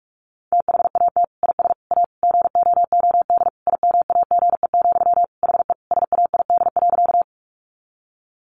Morse code, heard as a keyed tone, about 44 wpm, 700 Hz. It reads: THAT IS A GOOD IMAGE, HE SAID.
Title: Sentences from Top 700 Words 45wpm